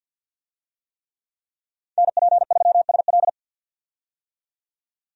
Morse code, heard as a keyed tone, about 49 wpm, 700 Hz. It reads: NP3SL
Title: USA Call Signs - 5 Characters - 50wpm